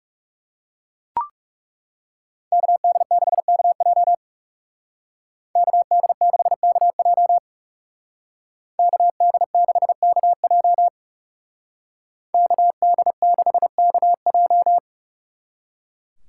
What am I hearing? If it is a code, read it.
Morse code, heard as a keyed tone, about 38 wpm, 700 Hz. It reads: KD6KJ KD6KJ KD6KJ KD6KJ